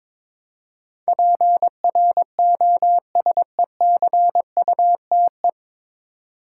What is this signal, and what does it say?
Morse code, heard as a keyed tone, about 22 wpm, 700 Hz. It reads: PROSECUTE
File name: 9 Letter Words - First 3 Letters Given - 22wpm